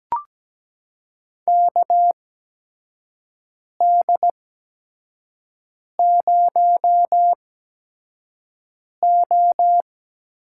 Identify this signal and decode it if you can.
Morse code, heard as a keyed tone, about 17 wpm, spaced as for 5 wpm, 700 Hz. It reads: KD0O
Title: USA Call Signs - 6 Characters - 8x Character Spacing - 17wpm